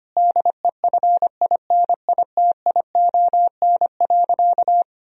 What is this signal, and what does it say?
Morse code, heard as a keyed tone, about 25 wpm, 700 Hz. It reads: DEFINITION.
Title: Sentences from Top 1000 Words - Review Part 4 - 25wpm